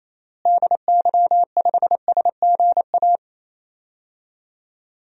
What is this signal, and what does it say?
Morse code, heard as a keyed tone, about 28 wpm, 700 Hz. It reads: DY5SGA
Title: DXCC-100 Call Signs - 6 Characters - Repeated 2x - 28wpm